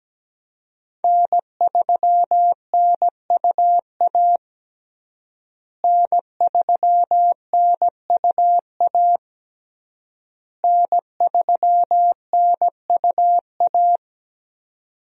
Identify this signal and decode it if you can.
Morse code, heard as a keyed tone, about 17 wpm, 700 Hz. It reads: N3NUA N3NUA N3NUA
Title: USA Call Signs - 5 Characters - Repeated 3x - 17wpm